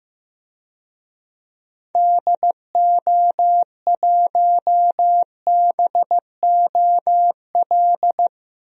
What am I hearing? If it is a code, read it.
Morse code, heard as a keyed tone, about 15 wpm, 700 Hz. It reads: DO1BOL
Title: DXCC-100 Call Signs - 6 Characters - Repeated 2x - 15wpm